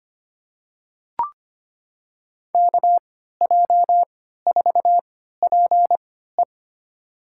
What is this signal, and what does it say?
Morse code, heard as a keyed tone, about 25 wpm, spaced as for 14 wpm, 700 Hz. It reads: KJ4PE
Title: USA Call Signs - 3x Character Spacing - 25wpm